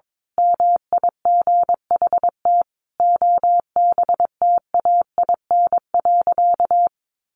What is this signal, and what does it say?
Morse code, heard as a keyed tone, about 22 wpm, 700 Hz. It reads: MIGHT OBTAIN.
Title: Sentences from Top 1300 Words 22wpm